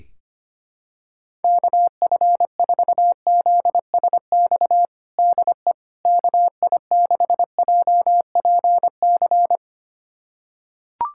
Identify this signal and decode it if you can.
Morse code, heard as a keyed tone, about 25 wpm, 700 Hz. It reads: KF4ZSX DE KI6JPC
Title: USA Callsign DE Callsign 25wpm